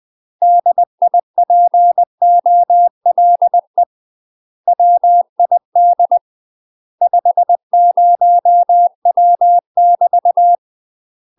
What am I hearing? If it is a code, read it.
Morse code, heard as a keyed tone, about 20 wpm, 700 Hz. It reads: DIPOLE WID 50W <BT>
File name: MASTER QSOs TRAINING/20WPM